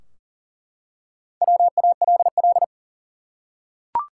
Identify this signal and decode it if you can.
Morse code, heard as a keyed tone, about 39 wpm, 700 Hz. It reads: WALL
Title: Four Letter Words - 40wpm